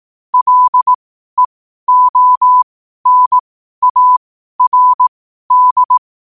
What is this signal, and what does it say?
Morse code, heard as a keyed tone, about 18 wpm, spaced as for 12 wpm, 1000 Hz. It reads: LEONARD
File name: qotdfiles / 12wpm